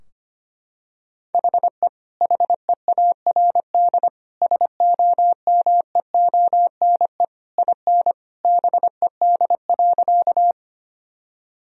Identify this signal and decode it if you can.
Morse code, heard as a keyed tone, about 25 wpm, 700 Hz. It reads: HE HEARD SOMEONE IN BED.
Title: Sentences from Top 1000 Words - Review Part 4 - 25wpm